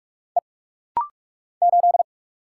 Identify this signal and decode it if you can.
Morse code, heard as a keyed tone, about 46 wpm, 700 Hz. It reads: E 8
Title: Single Letter-Number Alt - Rapid-Fire 45wpm